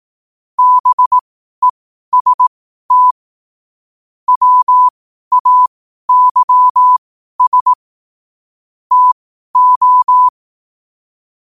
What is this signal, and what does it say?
Morse code, heard as a keyed tone, about 18 wpm, spaced as for 12 wpm, 1000 Hz. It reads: BEST WAYS TO
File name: qotdfiles / 12wpm